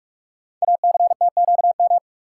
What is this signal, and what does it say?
Morse code, heard as a keyed tone, about 44 wpm, 690 Hz. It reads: ACTQM
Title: Code Groups - 5 Letters - 45wpm